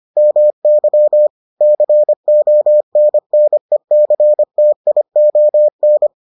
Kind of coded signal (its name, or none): Morse code